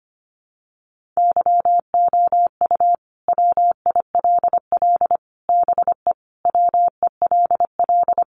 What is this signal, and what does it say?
Morse code, heard as a keyed tone, about 25 wpm, 700 Hz. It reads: YOU WILL BE WELL
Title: Sentences from Top 1000 Words 25wpm